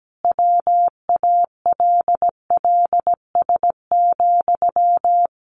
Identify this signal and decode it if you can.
Morse code, heard as a keyed tone, about 17 wpm, 700 Hz. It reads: WALLS,
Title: A Sound Of Thunder 17wpm